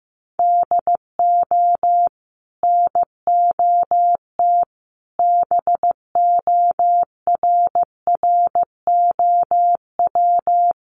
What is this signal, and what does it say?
Morse code, heard as a keyed tone, about 15 wpm, 700 Hz. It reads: DO NOT BORROW
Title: Sentences from Top 1500 Words 15wpm